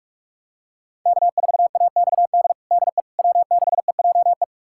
Morse code, heard as a keyed tone, about 42 wpm, 700 Hz. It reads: K4AXD DE W6EJE
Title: USA Callsign DE Callsign 45wpm